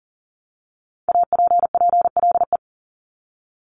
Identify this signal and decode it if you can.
Morse code, heard as a keyed tone, about 40 wpm, 700 Hz. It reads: APPLE
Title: Multi-letter Phonemes Repeated 3x 40wpm